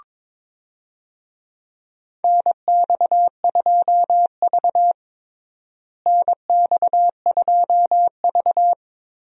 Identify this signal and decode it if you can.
Morse code, heard as a keyed tone, about 22 wpm, 700 Hz. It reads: NX2V NX2V
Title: USA Call Signs - 4 Characters - Repeated 3x - 22wpm